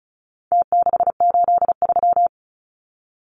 Morse code, heard as a keyed tone, about 34 wpm, 700 Hz. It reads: T683
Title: Numbers - Contest Style - 4-Digit 35wpm